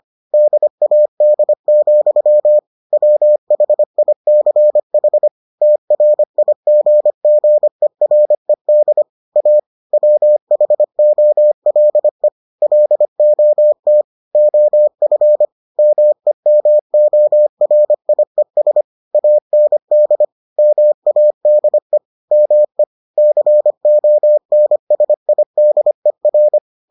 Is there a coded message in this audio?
Yes: Morse code from 0.3 to 26.6 s, heard as a keyed tone, about 25 wpm, 600 Hz. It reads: DAD, WHICH TRIGGERED A WHOLE LOT OF MEMORIES AND MADE ME CONSIDER